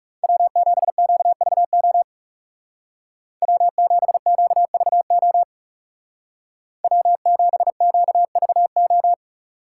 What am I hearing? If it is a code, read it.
Morse code, heard as a keyed tone, about 36 wpm, 700 Hz. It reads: W7QVO W7QVO W7QVO
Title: Speed Racing - USA 1x3 Call Signs 30wpm